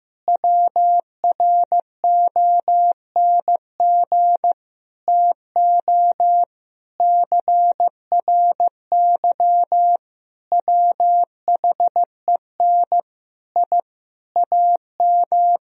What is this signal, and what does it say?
Morse code, heard as a keyed tone, about 15 wpm, 700 Hz. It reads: WRONG TO CRY WHEN I AM
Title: Sentences from Top 1000 Words - Review Part 2 - 15wpm